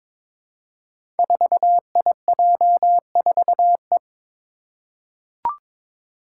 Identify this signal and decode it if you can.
Morse code, heard as a keyed tone, about 22 wpm, 700 Hz. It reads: 4IJ4E
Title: Five Challenging Letter-Number Combinations 22wpm